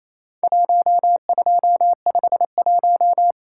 Morse code, heard as a keyed tone, about 28 wpm, 700 Hz. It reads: 1251
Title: Worked All Europe DX Contest - QTC Exchange - 28wpm